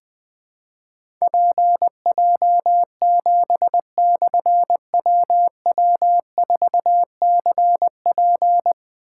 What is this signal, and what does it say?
Morse code, heard as a keyed tone, about 20 wpm, 700 Hz. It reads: PJ7/WW4CP
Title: Difficult Call Signs 20wpm